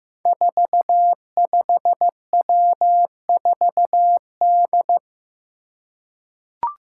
Morse code, heard as a keyed tone, about 15 wpm, 700 Hz. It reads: 45W4D
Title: Five Challenging Letter-Number Combinations 15wpm